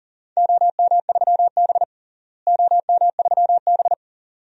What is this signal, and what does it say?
Morse code, heard as a keyed tone, about 38 wpm, 700 Hz. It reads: OM3B OM3B
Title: DXCC-100 Call Signs - 4 Characters - Repeated 2x - 40wpm